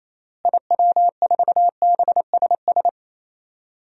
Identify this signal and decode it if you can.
Morse code, heard as a keyed tone, about 28 wpm, 700 Hz. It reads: IW4BSS
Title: DXCC-100 Call Signs - 6 Characters - Repeated 3x - 28wpm